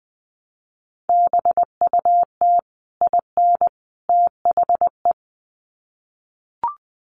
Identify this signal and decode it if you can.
Morse code, heard as a keyed tone, about 20 wpm, 700 Hz. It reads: BUT IN THE
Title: Sets of 3 Words 20wpm